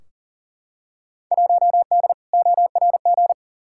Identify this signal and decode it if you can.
Morse code, heard as a keyed tone, about 40 wpm, 700 Hz. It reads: 1D ORG
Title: Field Day Exchanges - 40wpm